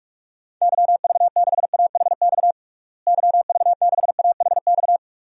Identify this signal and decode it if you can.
Morse code, heard as a keyed tone, about 46 wpm, 700 Hz. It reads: YV6AHX YV6AHX
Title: DXCC-100 Call Signs - 6 Characters - Repeated 3x - 45wpm